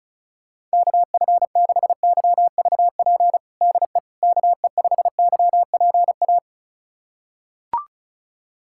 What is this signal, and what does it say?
Morse code, heard as a keyed tone, about 34 wpm, 700 Hz. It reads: KF6YVP DE KE5YPA
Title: USA Callsign DE Callsign 35wpm